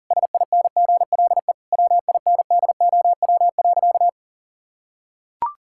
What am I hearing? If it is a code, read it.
Morse code, heard as a keyed tone, about 40 wpm, 700 Hz. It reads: SINGLE WINDOW.
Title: Sentences from Top 1600 Words 40wpm